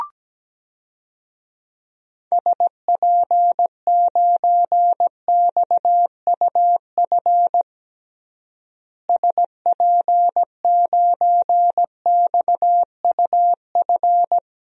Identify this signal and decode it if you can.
Morse code, heard as a keyed tone, about 17 wpm, 700 Hz. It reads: SP9XUF SP9XUF
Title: DXCC-100 Call Signs - 6 Characters - Repeated 2x - 17wpm